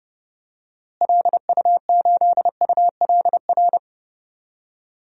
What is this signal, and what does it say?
Morse code, heard as a keyed tone, about 30 wpm, 700 Hz. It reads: LU8ULR
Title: DXCC-100 Call Signs - 6 Characters - Repeated 3x - 30wpm